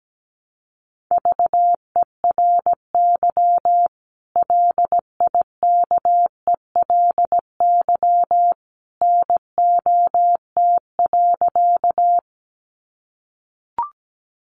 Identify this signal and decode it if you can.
Morse code, heard as a keyed tone, about 17 wpm, 700 Hz. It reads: VERY LIKELY NOT.
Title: Sentences from Top 600 Words 17wpm